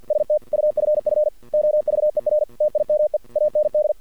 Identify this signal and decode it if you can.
Morse code, heard as a keyed tone, about 48 wpm, 600 Hz. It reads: STUFF OVER TIME AND
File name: archived / CRSnet